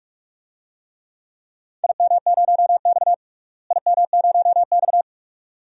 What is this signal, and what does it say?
Morse code, heard as a keyed tone, about 46 wpm, 700 Hz. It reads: IM0X IM0X
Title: DXCC-100 Call Signs - 4 Characters - Repeated 2x - 45wpm